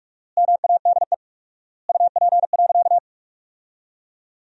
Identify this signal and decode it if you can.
Morse code, heard as a keyed tone, about 41 wpm, 700 Hz. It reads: MADE UP.
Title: 4x Word-Spacing - Sentences from Top 500 Words - 45wpm